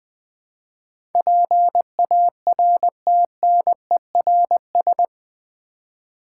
Morse code, heard as a keyed tone, about 20 wpm, 700 Hz. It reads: PARTNERS
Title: Build A Word - 4 Letters At A Time - 11-Letter Words - 20wpm